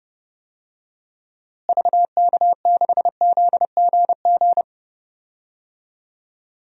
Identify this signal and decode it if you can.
Morse code, heard as a keyed tone, about 29 wpm, 700 Hz. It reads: VK6ZGG
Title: DXCC-100 Call Signs - 6 Characters - Repeated 2x - 30wpm